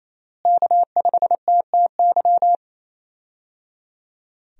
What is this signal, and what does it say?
Morse code, heard as a keyed tone, about 28 wpm, 700 Hz. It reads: K5TTY
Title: USA Call Signs - 5 Characters - Repeated 3x - 28wpm